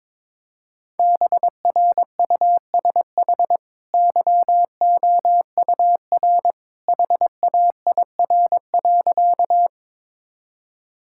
Morse code, heard as a keyed tone, about 22 wpm, 700 Hz. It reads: BRUSH YOUR HAIR.
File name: practice / 22wpm / Easy Sentences with Hints - 22wpm